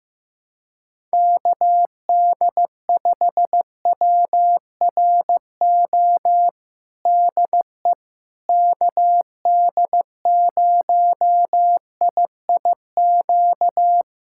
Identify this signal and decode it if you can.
Morse code, heard as a keyed tone, about 14 wpm, 700 Hz. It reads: KD5WRO DE KD0IIQ
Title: USA Callsign DE Callsign - 6 Characters - 15wpm